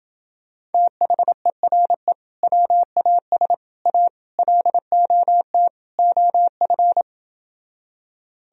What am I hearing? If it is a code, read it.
Morse code, heard as a keyed tone, about 28 wpm, 700 Hz. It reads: THERE WAS A LOT OF